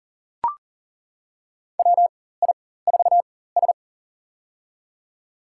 Morse code, heard as a keyed tone, about 38 wpm, spaced as for 18 wpm, 700 Hz. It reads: WI4S
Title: USA Call Signs - 4 Characters - 4x Character Spacing - 40wpm